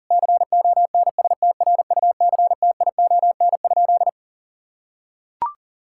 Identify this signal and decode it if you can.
Morse code, heard as a keyed tone, about 37 wpm, 700 Hz. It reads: CONSTRUCTION?
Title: Sentences from Top 1600 Words 40wpm